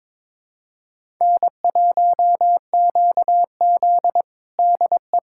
Morse code, heard as a keyed tone, about 22 wpm, 700 Hz. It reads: N1QZ DE
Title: USA Callsign DE Callsign - 4 Characters - 22wpm